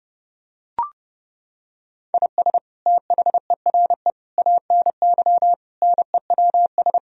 Morse code, heard as a keyed tone, about 31 wpm, 700 Hz. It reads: IS THERE ANY NEWS
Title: Sentences from Top 1000 Words - Review Part 4 - 30wpm